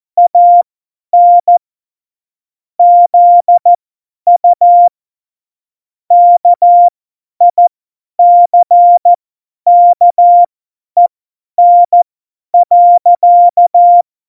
Morse code, heard as a keyed tone, about 14 wpm, spaced as for 10 wpm, 700 Hz. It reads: AN ZU KICKEN.
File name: Morsekurs / Witze / Witz 05. Januar 2021, 50 F70